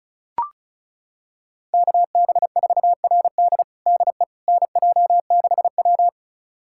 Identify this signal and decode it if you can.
Morse code, heard as a keyed tone, about 36 wpm, 700 Hz. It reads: KB4RD DE NJ6W
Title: USA Callsign DE Callsign 35wpm